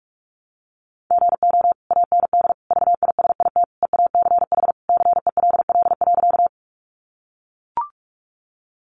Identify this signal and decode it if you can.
Morse code, heard as a keyed tone, about 44 wpm, 700 Hz. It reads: GO AND VISIT EACH CELL.
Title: Sentences from Top 1100 Words 45wpm